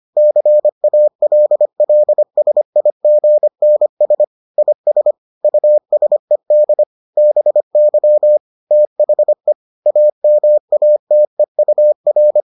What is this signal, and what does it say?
Morse code, heard as a keyed tone, about 25 wpm, 600 Hz. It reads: CALLSIGNS IS USED BY THE AMATEUR